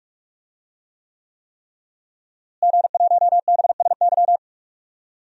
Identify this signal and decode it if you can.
Morse code, heard as a keyed tone, about 42 wpm, 700 Hz. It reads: G1BSY
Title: DXCC-100 Call Signs - 5 Characters - Repeated 3x - 45wpm